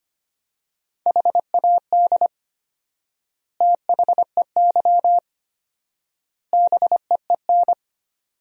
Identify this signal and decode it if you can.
Morse code, heard as a keyed tone, about 25 wpm, 700 Hz. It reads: HAD THEY BEEN
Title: 4x Word-Spacing - Sentences from Top 400 Words - 25wpm